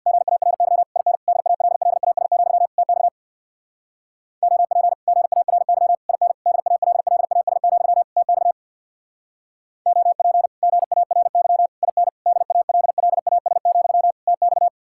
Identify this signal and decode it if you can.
Morse code, heard as a keyed tone, about 64 wpm, 700 Hz. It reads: GARY IN DALLAS, TX OP GARY IN DALLAS, TX OP GARY IN DALLAS, TX